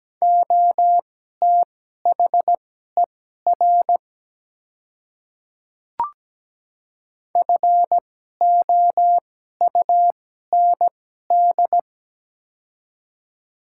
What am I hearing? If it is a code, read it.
Morse code, heard as a keyed tone, about 17 wpm, spaced as for 12 wpm, 700 Hz. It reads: OTHER FOUND
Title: 2x Character Spacing - Top 300 Words - 17wpm